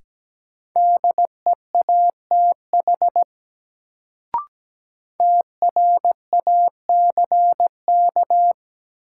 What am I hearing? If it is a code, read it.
Morse code, heard as a keyed tone, about 17 wpm, 700 Hz. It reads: DEATH TRACK
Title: Sentences from Top 1100 Words - New Word Drill - 17wpm